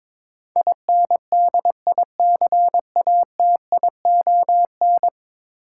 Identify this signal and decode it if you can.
Morse code, heard as a keyed tone, about 22 wpm, 700 Hz. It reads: INDICATION
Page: morse-code-ninja-practice-22wpm